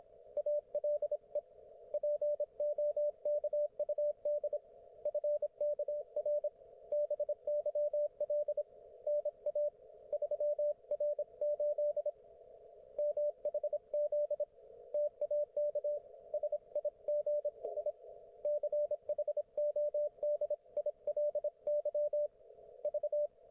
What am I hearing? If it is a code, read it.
Morse code, heard as a keyed tone, about 25 wpm, 590 Hz, with radio noise. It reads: ALE POKUD FKR BYL NA 3R8 MHZ TAK SIGI CHODILY V
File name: Archiv / CUC194 / ok1fhi cul194